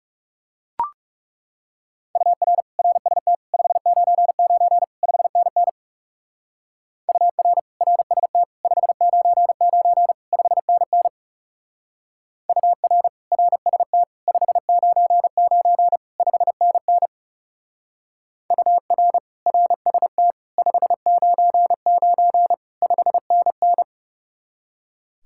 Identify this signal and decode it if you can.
Morse code, heard as a keyed tone, about 36 wpm, 700 Hz. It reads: UR RST 599 5NN UR RST 599 5NN UR RST 599 5NN UR RST 599 5NN
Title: Speed-Racing Common QSO Phrases - 30wpm